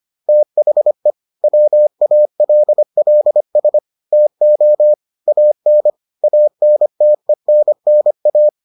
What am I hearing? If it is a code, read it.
Morse code, heard as a keyed tone, about 25 wpm, 600 Hz. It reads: THE WALLS TO AN ANTENNA